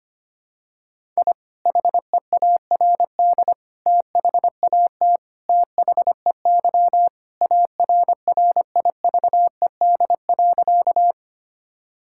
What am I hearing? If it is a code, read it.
Morse code, heard as a keyed tone, about 25 wpm, 700 Hz. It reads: I HEARD THAT THEY ARRIVED.
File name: practice / 25wpm / Sentences with 2-Word Phrases - 25wpm